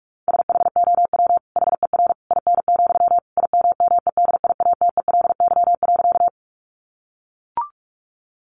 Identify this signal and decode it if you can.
Morse code, heard as a keyed tone, about 44 wpm, 700 Hz. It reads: SHOW HER IN, IMMEDIATELY.
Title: Sentences from Top 1100 Words 45wpm